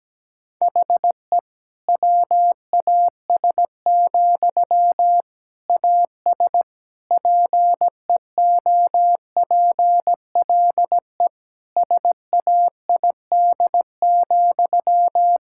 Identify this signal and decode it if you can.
Morse code, heard as a keyed tone, about 17 wpm, 700 Hz. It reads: HE WAS, AS PEOPLE SAID,
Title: Sentences from Top 1000 Words - Review Part 6 - 17wpm